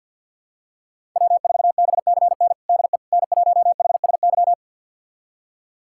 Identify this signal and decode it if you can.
Morse code, heard as a keyed tone, about 53 wpm, 700 Hz. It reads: W4BCN DE N1HSY
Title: USA Callsign DE Callsign - 5 Characters - 50wpm